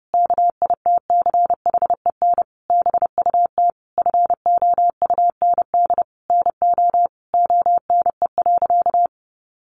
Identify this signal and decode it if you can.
Morse code, heard as a keyed tone, about 30 wpm, 700 Hz. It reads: KITCHEN BUT FOUND NO ONE.
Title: Sentences from Top 1400 Words 30wpm